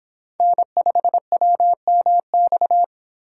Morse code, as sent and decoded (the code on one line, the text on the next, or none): -. ..... .-- -- -..-
N5WMX